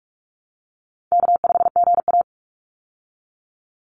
Morse code, heard as a keyed tone, about 44 wpm, 700 Hz. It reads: K5GA
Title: Easy Call Signs Repeated 3x - 45wpm